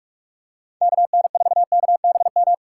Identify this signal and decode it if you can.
Morse code, heard as a keyed tone, about 42 wpm, 700 Hz. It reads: KN4KBM
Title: USA Call Signs - 6 Characters - 4x Character Spacing - 45wpm